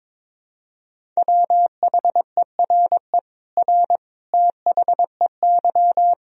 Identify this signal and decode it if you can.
Morse code, heard as a keyed tone, about 22 wpm, 700 Hz. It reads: WHERE R THEY
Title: Sentences from Top 200 Words and Abbreviations 22wpm